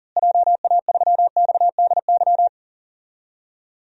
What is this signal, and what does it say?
Morse code, heard as a keyed tone, about 40 wpm, 700 Hz. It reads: JA3XDY
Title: DXCC-100 Call Signs - 6 Characters - Repeated 2x - 40wpm